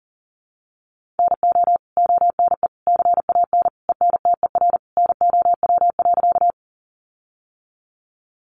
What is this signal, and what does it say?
Morse code, heard as a keyed tone, about 40 wpm, 700 Hz. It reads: NO ONE CAN ENTER NOW.
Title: Sentences from Top 1300 Words 40wpm